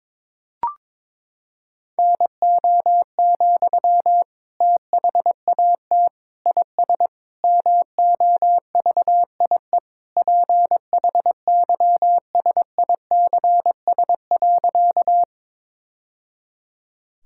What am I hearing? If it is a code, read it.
Morse code, heard as a keyed tone, about 22 wpm, 700 Hz. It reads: NO, THAT IS MOVIE PHYSICS.
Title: Sentences from Top 1000 Words - Review Part 5 - 22wpm